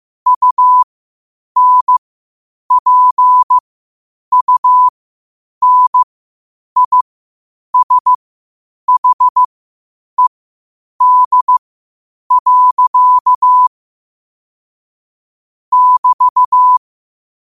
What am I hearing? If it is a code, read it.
Morse code, heard as a keyed tone, about 15 wpm, spaced as for 8 wpm, 1000 Hz. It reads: UNPUNISHED. <BT>
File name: qotdfiles / 8wpm